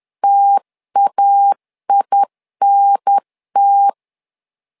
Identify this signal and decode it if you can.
Morse code, heard as a keyed tone, about 10 wpm, 800 Hz. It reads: TAINT
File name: LEZIONI / 2 UHDN / 2 UHDN 10wpm 1904